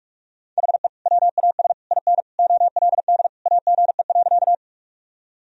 Morse code, heard as a keyed tone, about 43 wpm, 700 Hz. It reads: HE WAS IN OLD AGE.
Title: Sentences from Top 1000 Words - Review Part 3 - 45wpm